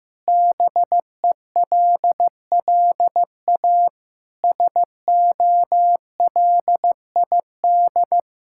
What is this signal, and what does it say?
Morse code, heard as a keyed tone, about 15 wpm, 700 Hz. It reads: BELLA SOLID
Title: K1USN SST QSO Practice 15wpm